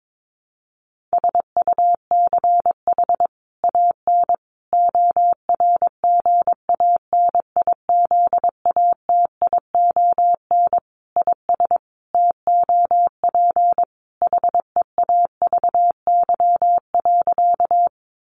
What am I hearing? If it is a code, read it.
Morse code, heard as a keyed tone, about 22 wpm, 700 Hz. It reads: SUCH AN ORGANIZATION IS TOP HEAVY.
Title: Sentences from Top 1100 Words 22wpm